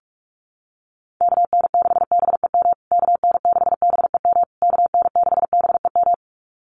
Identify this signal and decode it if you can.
Morse code, heard as a keyed tone, about 44 wpm, 690 Hz. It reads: KN6BEM KN6BEM KN6BEM
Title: USA Call Signs Repeated 3x 45wpm